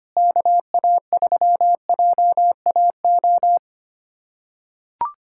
Morse code, heard as a keyed tone, about 25 wpm, 700 Hz. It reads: KA3JAO
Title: USA Callsign DE Callsign 25wpm